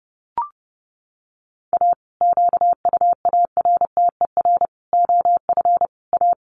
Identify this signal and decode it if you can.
Morse code, heard as a keyed tone, about 30 wpm, 700 Hz. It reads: A QUARTER OF A
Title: Sets of 5 Words Encore2 30wpm